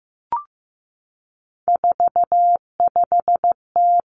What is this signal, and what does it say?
Morse code, heard as a keyed tone, about 15 wpm, 700 Hz. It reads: 45T
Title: Three Challenging Letter-Number Combinations 15wpm